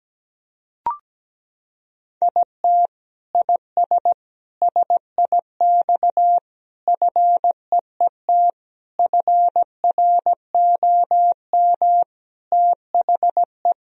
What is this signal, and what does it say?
Morse code, heard as a keyed tone, about 17 wpm, 700 Hz. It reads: IT IS SIX FEET FROM THE
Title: Sentences from Top 1000 Words - Review Part 2 - 17wpm